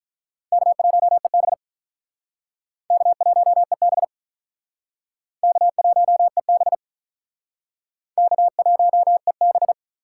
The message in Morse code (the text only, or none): K1EB K1EB K1EB K1EB